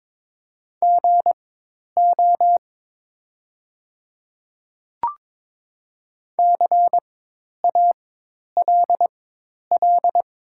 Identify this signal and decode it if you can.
Morse code, heard as a keyed tone, about 22 wpm, spaced as for 10 wpm, 700 Hz. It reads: GO CALL
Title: 4x Character Spacing - Top 100 Words - 22wpm